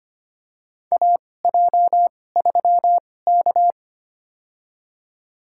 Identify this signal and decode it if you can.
Morse code, heard as a keyed tone, about 25 wpm, spaced as for 18 wpm, 700 Hz. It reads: AJ3K
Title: USA Call Signs - 2x Character Spacing - 25wpm